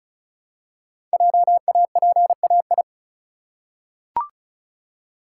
Morse code, heard as a keyed tone, about 36 wpm, 700 Hz. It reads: JAPAI
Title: Code Groups - 5 Letter:Numbers - 35wpm